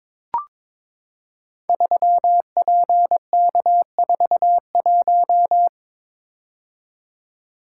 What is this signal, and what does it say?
Morse code, heard as a keyed tone, about 22 wpm, 700 Hz. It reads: 3PK41
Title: Code Groups - 5 Letter:Numbers - 22wpm